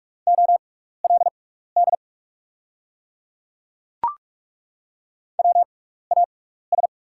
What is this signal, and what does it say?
Morse code, heard as a keyed tone, about 42 wpm, spaced as for 15 wpm, 690 Hz. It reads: OLD WAS
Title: 6x Character Spacing - Top 300 Words - 45wpm